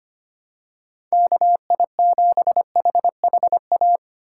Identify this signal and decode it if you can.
Morse code, heard as a keyed tone, about 25 wpm, 700 Hz. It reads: KI7HHA
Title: USA Call Signs - 6 Characters - 6x Character Spacing - 25wpm